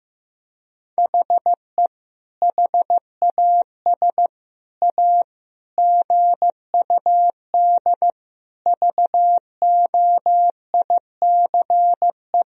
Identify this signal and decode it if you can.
Morse code, heard as a keyed tone, about 15 wpm, 700 Hz. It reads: HE HAS A GUD VOICE
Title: Sentences from Top 500 Words and Abbreviations 15wpm